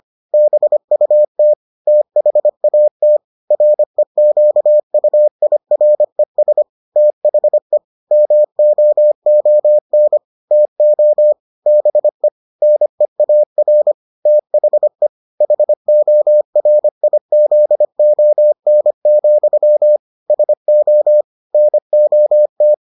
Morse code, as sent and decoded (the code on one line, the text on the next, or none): -... ..- - / - .... .- - / .-. --.- ..- .. .-. ... / - .... / -- --- --- -. / - --- / -... / -. .- .-. / - .... / .... --- .-. .. --.. --- -. --..-- / ... --- / -. --- -
BUT THAT REQUIRES THE MOON TO BE NEAR THE HORIZON, SO NOT